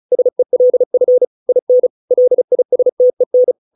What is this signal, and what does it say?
Morse code, heard as a keyed tone, about 34 wpm, 500 Hz. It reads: SELF IN LISTEN